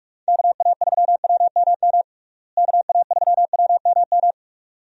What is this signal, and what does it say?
Morse code, heard as a keyed tone, about 46 wpm, 700 Hz. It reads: KA3WMM KA3WMM
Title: USA Call Signs - 6 Characters - Repeated 2x - 45wpm